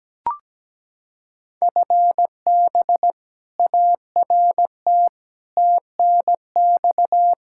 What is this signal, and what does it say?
Morse code, heard as a keyed tone, about 17 wpm, 700 Hz. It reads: FB ART TNX
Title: K1USN SST QSO Practice 17wpm